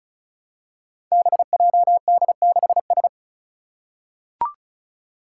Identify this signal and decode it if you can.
Morse code, heard as a keyed tone, about 36 wpm, 700 Hz. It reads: BJD6S
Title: Five Challenging Letter-Number Combinations 35wpm